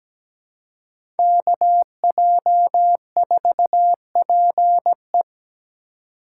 Morse code, heard as a keyed tone, about 17 wpm, 700 Hz. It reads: KJ4PE
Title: USA Call Signs - 5 Characters - 2x Character Spacing - 17wpm